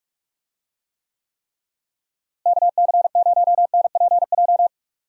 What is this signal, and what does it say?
Morse code, heard as a keyed tone, about 46 wpm, 700 Hz. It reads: KC0NPJ
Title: USA Call Signs - 6 Characters - Repeated 3x - 45wpm